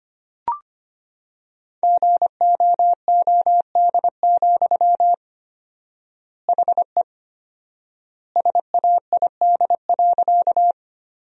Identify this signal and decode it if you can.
Morse code, heard as a keyed tone, about 25 wpm, 700 Hz. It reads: GOOD, HE SAID.
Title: 4x Word-Spacing - Sentences from Top 200 Words - 25wpm